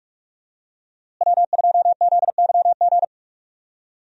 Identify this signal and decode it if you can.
Morse code, heard as a keyed tone, about 46 wpm, 700 Hz. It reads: W2ZYG
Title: USA Call Signs - 5 Characters - Repeated 2x - 45wpm